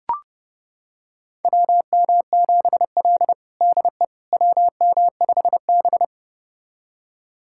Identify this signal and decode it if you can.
Morse code, heard as a keyed tone, about 30 wpm, 700 Hz. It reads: WM7L DE WM5B
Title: USA Callsign DE Callsign - 4 Characters - 30wpm